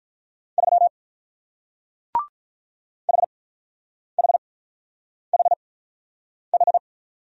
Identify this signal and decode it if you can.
Morse code, heard as a keyed tone, about 46 wpm, spaced as for 9 wpm, 700 Hz. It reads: 3 HHHH